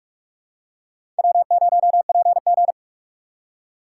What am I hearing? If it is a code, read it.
Morse code, heard as a keyed tone, about 46 wpm, 700 Hz. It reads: W0PG
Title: USA Call Signs - 3x Character Spacing - 45wpm